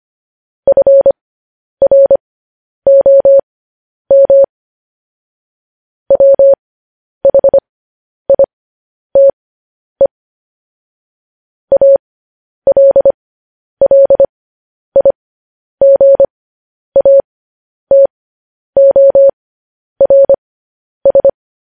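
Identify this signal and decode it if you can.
Morse code, heard as a keyed tone, about 25 wpm, spaced as for 10 wpm, 550 Hz. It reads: FROM WHITE ALLIGATORS
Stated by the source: Morse code news headlines broadcast.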